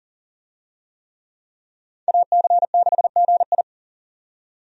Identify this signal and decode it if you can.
Morse code, heard as a keyed tone, about 37 wpm, 700 Hz. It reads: AC6GI
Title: USA Call Signs - 5 Characters - 40wpm